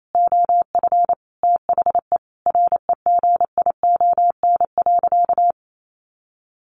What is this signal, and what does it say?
Morse code, heard as a keyed tone, about 28 wpm, 700 Hz. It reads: OF THE REGION.
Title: Sentences from Top 1500 Words 28wpm